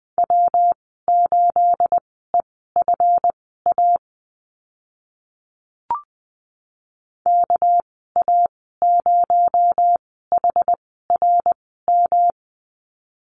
Morse code, heard as a keyed tone, about 20 wpm, spaced as for 14 wpm, 700 Hz. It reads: W8EFA KA0HRM
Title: 231 Lesson 37g - Letter x - Call Signs 20wpm